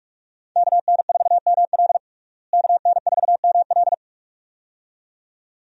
Morse code, heard as a keyed tone, about 46 wpm, 700 Hz. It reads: KN4ML KN4ML
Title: USA Call Signs - 5 Characters - Repeated 2x - 45wpm